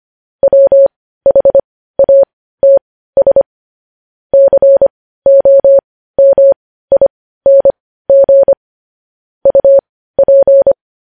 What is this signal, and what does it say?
Morse code, heard as a keyed tone, about 25 wpm, spaced as for 15 wpm, 550 Hz. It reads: WHATS COMING UP
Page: news-headlines-in-morse-code-at-15-wpm